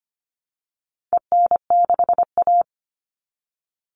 Morse code, heard as a keyed tone, about 25 wpm, 700 Hz. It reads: EN6A
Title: All Cut Numbers - 4-Digit 25wpm